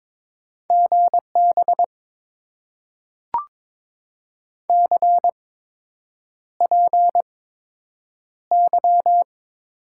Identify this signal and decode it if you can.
Morse code, heard as a keyed tone, about 22 wpm, spaced as for 8 wpm, 700 Hz. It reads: GB CPY
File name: Top 90 QSO Elements - 8x Character Spacing - 22wpm